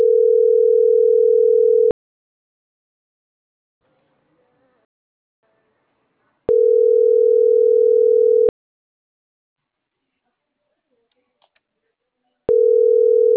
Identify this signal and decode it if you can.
Ringback tone (North America), 440 + 480 Hz.